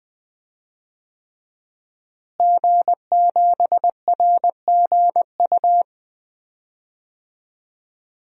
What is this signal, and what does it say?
Morse code, heard as a keyed tone, about 20 wpm, 700 Hz. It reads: G7RGU
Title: DXCC-100 Call Signs - 5 Characters - Repeated 3x - 20wpm